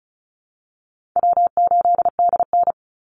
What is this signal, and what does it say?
Morse code, heard as a keyed tone, about 34 wpm, 700 Hz. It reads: W8DN
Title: Call Signs Repeated 2x - 35wpm